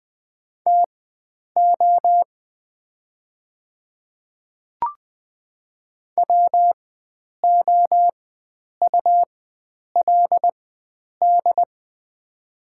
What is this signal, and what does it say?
Morse code, heard as a keyed tone, about 20 wpm, spaced as for 9 wpm, 700 Hz. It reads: TO WOULD
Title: 4x Character Spacing - Top 100 Words - 20wpm